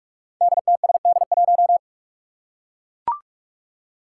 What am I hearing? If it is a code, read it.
Morse code, heard as a keyed tone, about 41 wpm, 700 Hz. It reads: DTSD1
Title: Five Challenging Letter-Number Combinations 45wpm